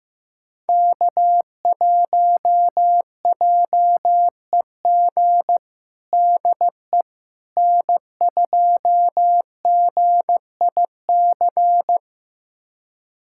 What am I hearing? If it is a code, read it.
Morse code, heard as a keyed tone, about 15 wpm, 700 Hz. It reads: K1JEG DE N2GIC
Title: USA Callsign DE Callsign 15wpm